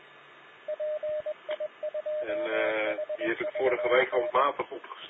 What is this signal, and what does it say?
Morse code, heard as a keyed tone, about 21 wpm, 610 Hz, with speech and radio noise. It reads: PI2HVN